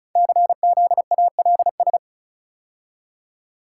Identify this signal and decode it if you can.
Morse code, heard as a keyed tone, about 36 wpm, 690 Hz. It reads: CZALS
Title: Code Groups - 5 Letters - 35wpm